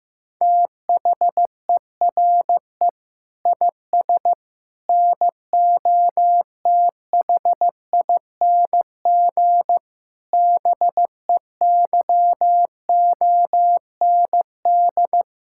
Morse code, heard as a keyed tone, about 16 wpm, 700 Hz. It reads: THERE IS NOTHING BEYOND
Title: Sentences from Top 1000 Words - Review Part 6 - 15wpm